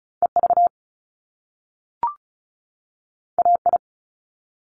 Morse code, heard as a keyed tone, about 36 wpm, 700 Hz. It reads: EV AI
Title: Top Two Letter Combinations in Words 35wpm